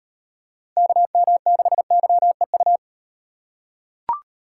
Morse code, heard as a keyed tone, about 38 wpm, 700 Hz. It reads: KM6YEU